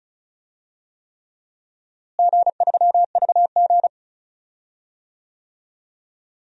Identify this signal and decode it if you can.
Morse code, heard as a keyed tone, about 34 wpm, 700 Hz. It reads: G3VG
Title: DXCC-100 Call Signs - 4 Characters - Repeated 3x - 35wpm